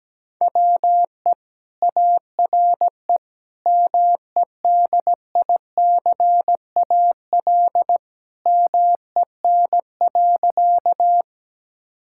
Morse code, heard as a keyed tone, about 17 wpm, 700 Hz. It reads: WE ARE MEDICAL MEN.
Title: Sentences from Top 1000 Words - Review Part 5 - 17wpm